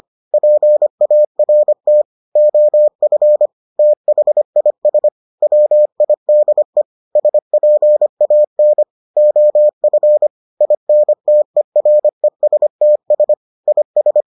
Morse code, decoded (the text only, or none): PART OF THIS WIDE SPAN OF INTERESTS IS